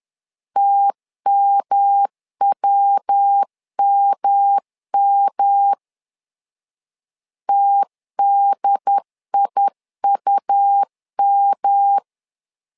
Morse code, heard as a keyed tone, about 10 wpm, 800 Hz. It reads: TMWMM TDIUM
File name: LEZIONI / 3 MWRO / 3 MW 10wpm 1959